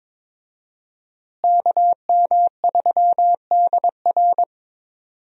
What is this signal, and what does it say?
Morse code, heard as a keyed tone, about 23 wpm, 700 Hz. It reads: KM3DR
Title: USA Call Signs - 5 Characters - Repeated 3x - 22wpm